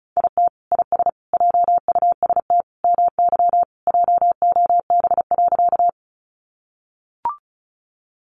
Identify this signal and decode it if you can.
Morse code, heard as a keyed tone, about 34 wpm, 700 Hz. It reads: IT IS JUST MY JOB.
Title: Sentences from Top 600 Words 35wpm